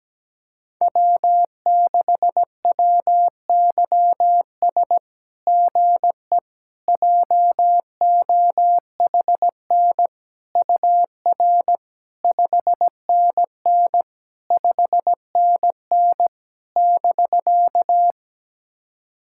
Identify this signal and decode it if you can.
Morse code, heard as a keyed tone, about 17 wpm, 700 Hz. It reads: W6WYS GE JOHN UR 5NN 5NN <BK>
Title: SOTA QSO Practice 17wpm